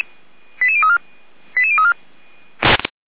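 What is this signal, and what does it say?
Five-tone selective call (ZVEI), heard as quick runs of five tones, tones of about 70 ms. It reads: 88034 88034
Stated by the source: ZVEI five-tone call received by SDR, feuerwehr-orlamuende.org (fire brigade Orlamünde).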